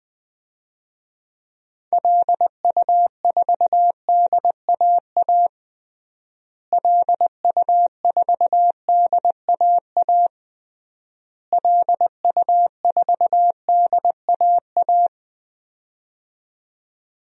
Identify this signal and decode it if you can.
Morse code, heard as a keyed tone, about 21 wpm, 700 Hz. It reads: LU4DAA LU4DAA LU4DAA
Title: DXCC-100 Call Signs - 6 Characters - Repeated 3x - 20wpm